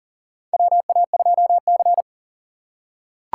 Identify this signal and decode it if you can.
Morse code, heard as a keyed tone, about 39 wpm, 700 Hz. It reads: WA2C
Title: USA Call Signs - 4 Characters - 4x Character Spacing - 40wpm